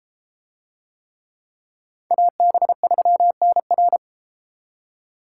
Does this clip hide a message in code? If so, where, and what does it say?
Yes: Morse code from 2.1 to 4.0 s, heard as a keyed tone, about 30 wpm, 700 Hz. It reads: AB3NR